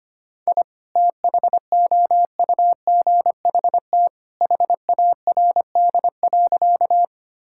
Morse code, heard as a keyed tone, about 25 wpm, 700 Hz. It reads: I THOUGHT HARD.
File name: practice / 25wpm / Sentences with 2-Word Phrases - 25wpm